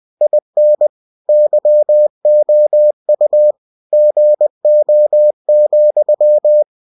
Morse code, heard as a keyed tone, about 20 wpm, 600 Hz. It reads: IN YOU GO,
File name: cw-books / winnie-the-pooh / winnie-20wpm-page-0069